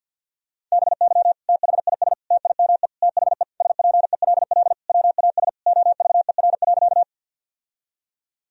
Morse code, heard as a keyed tone, about 52 wpm, 700 Hz. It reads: BY THIS TIME THE SPELL WAS OVER.
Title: Sentences from Top 1000 Words - Review Part 1 - 50wpm